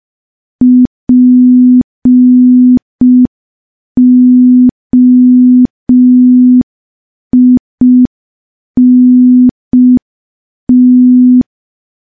Morse code, heard as a keyed tone, about 5 wpm, 250 Hz. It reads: POINT
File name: morse / morse-training-05WPM